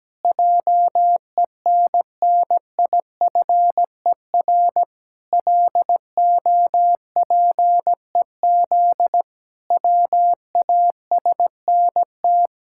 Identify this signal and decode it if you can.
Morse code, heard as a keyed tone, about 17 wpm, 700 Hz. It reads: JENNIFER LOPEZ WASNT
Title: Pangram Challenge v2 - 17wpm